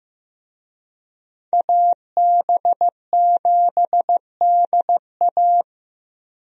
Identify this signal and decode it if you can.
Morse code, heard as a keyed tone, about 15 wpm, 700 Hz. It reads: AB7DA
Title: USA Call Signs - 5 Characters - Repeated 3x - 15wpm